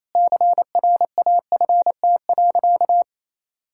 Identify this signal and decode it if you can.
Morse code, heard as a keyed tone, about 27 wpm, 700 Hz. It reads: CRAFT.
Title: Sentences from Top 1000 Words - Review Part 6 - 28wpm